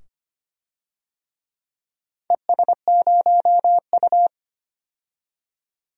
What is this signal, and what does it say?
Morse code, heard as a keyed tone, about 25 wpm, 700 Hz. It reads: ES0U
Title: DXCC-100 Call Signs - 4 Characters - 25wpm